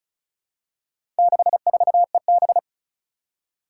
Morse code, heard as a keyed tone, about 36 wpm, 700 Hz. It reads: 64EB
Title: Four Challenging Letter-Number Combinations 35wpm